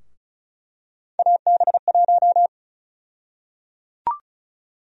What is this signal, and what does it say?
Morse code, heard as a keyed tone, about 36 wpm, 700 Hz. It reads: AB1
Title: USA Call Sign Prefixes - Repeated 2x - 35wpm